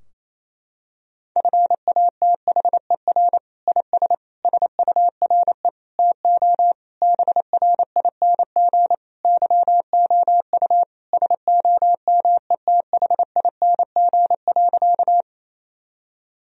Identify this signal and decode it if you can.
Morse code, heard as a keyed tone, about 28 wpm, 700 Hz. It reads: FATHER IS SURE TO BRING YOU SOMETHING.
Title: Sentences from Top 1000 Words - Review Part 3 - 28wpm